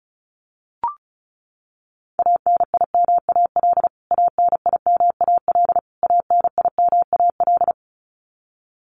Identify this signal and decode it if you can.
Morse code, heard as a keyed tone, about 34 wpm, 700 Hz. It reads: ANIMAL ANIMAL ANIMAL
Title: Top 200 Words Repeated 35wpm